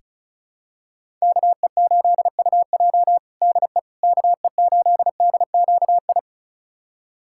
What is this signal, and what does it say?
Morse code, heard as a keyed tone, about 36 wpm, 700 Hz. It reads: KE8UJ DE KE8DQI
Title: USA Callsign DE Callsign 35wpm